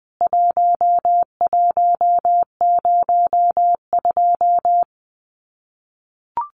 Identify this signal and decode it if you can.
Morse code, heard as a keyed tone, about 20 wpm, 700 Hz. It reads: 1102
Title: CWT Exchange 20wpm